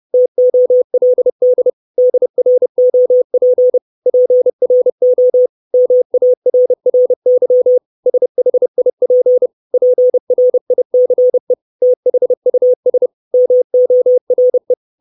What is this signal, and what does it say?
Morse code, heard as a keyed tone, about 30 wpm, 500 Hz. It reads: TOLD DROP PRO MARRY SHIP PRICE THUS MORE
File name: 632 Words 30WPM